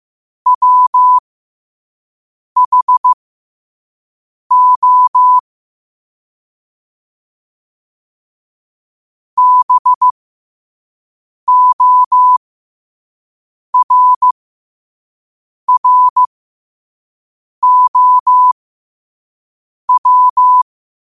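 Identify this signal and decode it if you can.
Morse code, heard as a keyed tone, about 15 wpm, spaced as for 5 wpm, 1000 Hz. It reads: WHO BORROW
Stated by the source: Morse code sent at 5 wpm.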